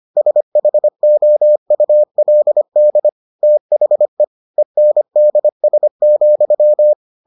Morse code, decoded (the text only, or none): SHOULD THE ENDS,